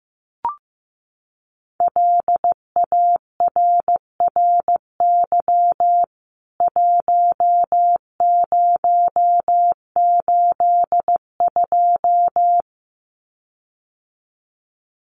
Morse code, heard as a keyed tone, about 15 wpm, 700 Hz. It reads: LARRY 1082
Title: CWT Exchange 15wpm